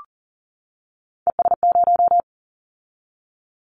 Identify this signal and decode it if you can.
Morse code, heard as a keyed tone, about 39 wpm, 700 Hz. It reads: ES0
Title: Callsign Prefixes for Top 50 DXCC - 40wpm